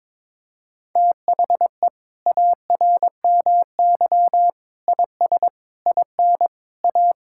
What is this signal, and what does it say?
Morse code, heard as a keyed tone, about 22 wpm, 700 Hz. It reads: THE ARMY IS IN A
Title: Sentences from Top 1000 Words - Review Part 6 - 22wpm